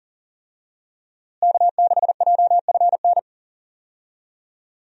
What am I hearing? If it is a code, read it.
Morse code, heard as a keyed tone, about 40 wpm, 700 Hz. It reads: K6JFN
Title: USA Call Signs - 5 Characters - Repeated 2x - 40wpm